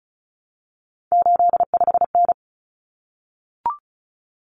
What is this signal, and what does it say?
Morse code, heard as a keyed tone, about 34 wpm, 700 Hz. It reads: 85N
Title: Common Cut Numbers - 3-Digit 35wpm